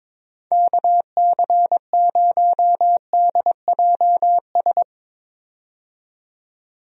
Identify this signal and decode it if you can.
Morse code, heard as a keyed tone, about 22 wpm, 700 Hz. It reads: KC0DJS
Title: USA Call Signs - 6 Characters - 22wpm